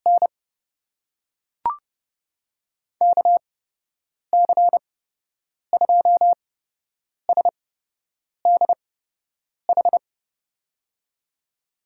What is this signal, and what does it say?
Morse code, heard as a keyed tone, about 30 wpm, spaced as for 8 wpm, 700 Hz. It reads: N KC2SDH
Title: USA Call Signs - 8x Character Spacing - 30wpm